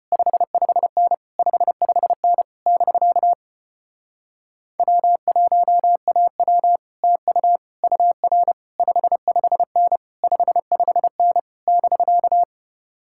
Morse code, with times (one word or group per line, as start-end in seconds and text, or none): 0.1-1.1 55N
1.4-2.4 55N
2.7-3.3 <BK>
4.8-6.8 W1AW
7.0-7.6 TU
7.8-8.5 UR
8.8-10.0 55N
10.2-11.4 55N
11.7-12.4 <BK>